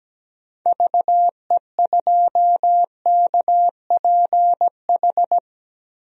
Morse code, heard as a keyed tone, about 17 wpm, 700 Hz. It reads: VE2KPH
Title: DXCC-100 Call Signs - 6 Characters - Repeated 2x - 17wpm